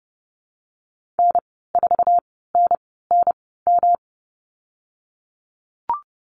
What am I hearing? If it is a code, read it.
Morse code, heard as a keyed tone, about 30 wpm, spaced as for 17 wpm, 700 Hz. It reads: N4NNM
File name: Lesson 19g - Letter w - Call Signs 30wpm